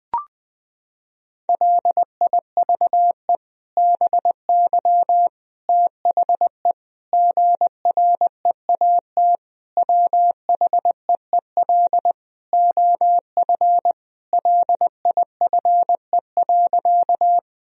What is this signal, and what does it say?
Morse code, heard as a keyed tone, about 20 wpm, 700 Hz. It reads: LIVE BY THE GREAT WHEEL OF LIFE.
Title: Sentences from Top 1000 Words - Review Part 3 - 20wpm